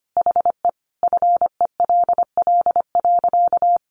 Morse code, heard as a keyed tone, about 25 wpm, 700 Hz. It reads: HE FELL.
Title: A Sound Of Thunder 25wpm